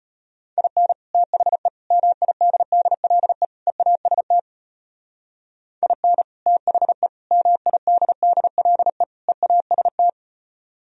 Morse code, heard as a keyed tone, about 32 wpm, 700 Hz. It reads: IN THE MIDDLE EAST IN THE MIDDLE EAST